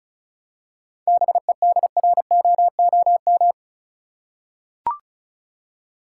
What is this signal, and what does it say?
Morse code, heard as a keyed tone, about 36 wpm, 700 Hz. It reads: BEDROOM
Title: Seven Letter Words - 35wpm